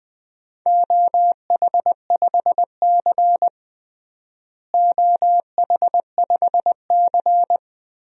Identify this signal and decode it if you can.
Morse code, heard as a keyed tone, about 20 wpm, 700 Hz. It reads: OH5C OH5C
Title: DXCC-100 Call Signs - 4 Characters - Repeated 2x - 20wpm